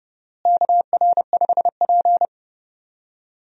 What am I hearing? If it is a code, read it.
Morse code, heard as a keyed tone, about 32 wpm, 700 Hz. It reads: KR5P
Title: USA Call Signs - 4 Characters - Repeated 3x - 30wpm